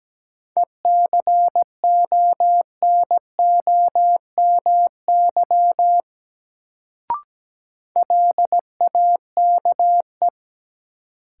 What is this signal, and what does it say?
Morse code, heard as a keyed tone, about 17 wpm, 700 Hz. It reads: ECONOMY LAKE
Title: Sentences from Top 1000 Words - Review Part 3 - New Word Drill - 17wpm